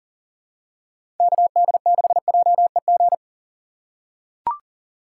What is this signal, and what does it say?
Morse code, heard as a keyed tone, about 40 wpm, 700 Hz. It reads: KD6JEG
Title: USA Call Signs - 6 Characters - 6x Character Spacing - 40wpm